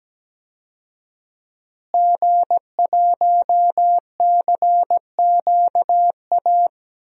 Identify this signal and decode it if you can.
Morse code, heard as a keyed tone, about 17 wpm, 700 Hz. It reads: G1CQA
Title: DXCC-100 Call Signs - 5 Characters - Repeated 3x - 17wpm